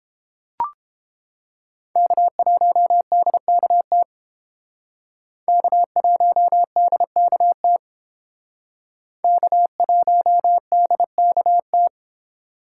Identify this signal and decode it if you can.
Morse code, heard as a keyed tone, about 30 wpm, 700 Hz. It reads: K1DKT K1DKT K1DKT